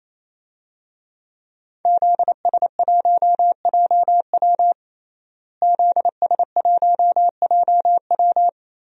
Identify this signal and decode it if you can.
Morse code, heard as a keyed tone, about 28 wpm, 700 Hz. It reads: ZS1JW ZS1JW
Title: DXCC-100 Call Signs - 5 Characters - Repeated 3x - 28wpm